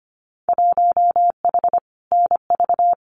Morse code, heard as a keyed tone, about 25 wpm, 700 Hz. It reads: 1H NV
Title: Winter Field Day Exchanges 25wpm